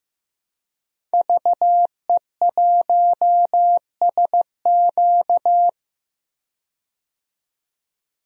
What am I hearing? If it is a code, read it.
Morse code, heard as a keyed tone, about 15 wpm, 700 Hz. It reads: VE1SQ
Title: DXCC-100 Call Signs - 5 Characters - 15wpm